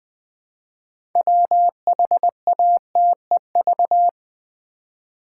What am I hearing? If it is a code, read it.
Morse code, heard as a keyed tone, about 20 wpm, 700 Hz. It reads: WHATEV
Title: Build A Word - 3 Letters At A Time - 8-Letter Words - 20wpm